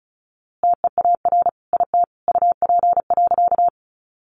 Morse code, heard as a keyed tone, about 34 wpm, 700 Hz. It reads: TEAR IT UP.
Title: A Sound Of Thunder 35wpm